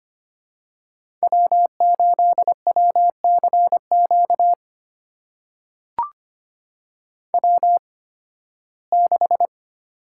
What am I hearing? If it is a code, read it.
Morse code, heard as a keyed tone, about 25 wpm, spaced as for 16 wpm, 700 Hz. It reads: W8WCQ W6
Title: USA Call Signs - 5 Characters - 8x Character Spacing - 25wpm